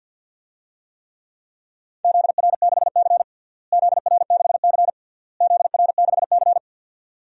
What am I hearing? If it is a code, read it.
Morse code, heard as a keyed tone, about 49 wpm, 700 Hz. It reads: ZR6C ZR6C ZR6C
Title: DXCC-100 Call Signs - 4 Characters - Repeated 3x - 50wpm